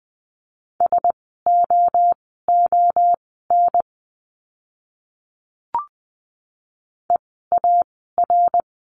Morse code, heard as a keyed tone, about 20 wpm, spaced as for 14 wpm, 700 Hz. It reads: SOON EAR
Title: 054 Lesson 10f - Letter r - Words 20wpm